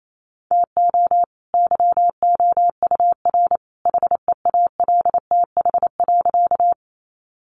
Morse code, heard as a keyed tone, about 28 wpm, 700 Hz. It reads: TO YOUR HEALTH.
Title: Sentences from Top 600 Words 28wpm